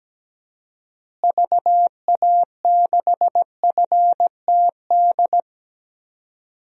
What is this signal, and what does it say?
Morse code, heard as a keyed tone, about 17 wpm, 700 Hz. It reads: VA6FTD
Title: DXCC-100 Call Signs - 6 Characters - Repeated 3x - 17wpm